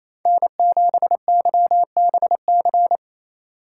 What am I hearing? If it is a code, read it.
Morse code, heard as a keyed tone, about 28 wpm, 700 Hz. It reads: N7YBC
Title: USA Call Signs - 5 Characters - 6x Character Spacing - 28wpm